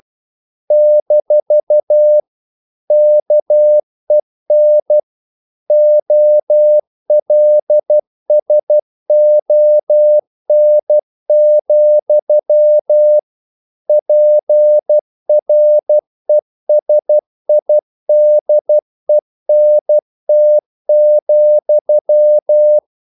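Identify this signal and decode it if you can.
Morse code, heard as a keyed tone, about 12 wpm, 600 Hz. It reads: - KEN OLSON, PRESIDENT,
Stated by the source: Morse practice sending English quotations.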